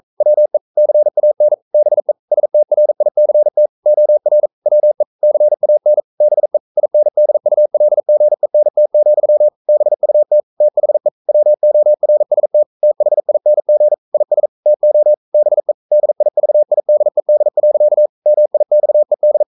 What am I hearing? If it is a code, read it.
Morse code, heard as a keyed tone, about 42 wpm, 600 Hz. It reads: WE CAN BE STRICT OR WE CAN BE INDULGENT, BUT THE WORST THING IS TO BE DIVIDED. MIXED
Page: Romanen Never af Ken Follet på engelsk.